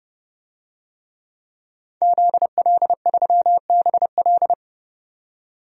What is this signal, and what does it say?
Morse code, heard as a keyed tone, about 30 wpm, 700 Hz. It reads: ZL3BL
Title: DXCC-100 Call Signs - 5 Characters - 30wpm